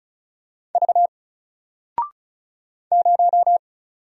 Morse code, heard as a keyed tone, about 36 wpm, 700 Hz. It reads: V 0
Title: Single Character (Letter:Number:Punctuation:Prosign) 35wpm